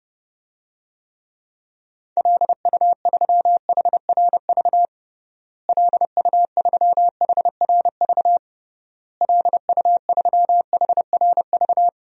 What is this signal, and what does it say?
Morse code, heard as a keyed tone, about 30 wpm, 700 Hz. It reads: LU3HRV LU3HRV LU3HRV
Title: DXCC-100 Call Signs - 6 Characters - Repeated 3x - 30wpm